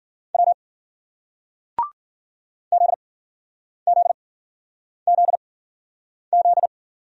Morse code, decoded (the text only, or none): W ZZZZ